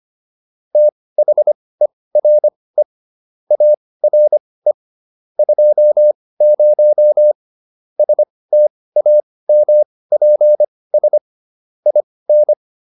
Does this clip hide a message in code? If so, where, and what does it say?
Yes: Morse code from 0.8 to 12.5 s, heard as a keyed tone, about 25 wpm, spaced as for 18 wpm, 600 Hz. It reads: THERE ARE 20 STAMPS IN